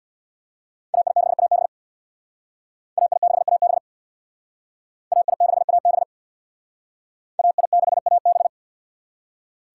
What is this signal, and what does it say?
Morse code, heard as a keyed tone, about 61 wpm, 700 Hz. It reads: AI6AB AI6AB AI6AB AI6AB
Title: Speed Racing - USA 2x2 Call Signs 50wpm